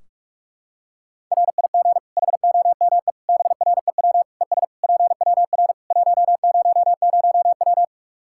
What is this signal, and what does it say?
Morse code, heard as a keyed tone, about 41 wpm, 700 Hz. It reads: RIG HOME BREW ES PWR 100W
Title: Practice QSO - Exchange 2 - Info - 45wpm